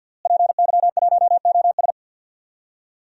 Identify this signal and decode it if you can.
Morse code, heard as a keyed tone, about 53 wpm, 700 Hz. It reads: PY1OS
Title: DXCC-100 Call Signs - 5 Characters - Repeated 3x - 50wpm